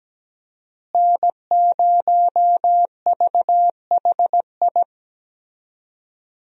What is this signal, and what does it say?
Morse code, heard as a keyed tone, about 17 wpm, 700 Hz. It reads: N0VHI
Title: USA Call Signs - 5 Characters - Repeated 2x - 17wpm